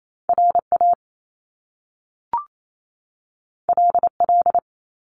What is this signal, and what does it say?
Morse code, heard as a keyed tone, about 28 wpm, 700 Hz. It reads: RA LL
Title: Top Two Letter Combinations in Words 28wpm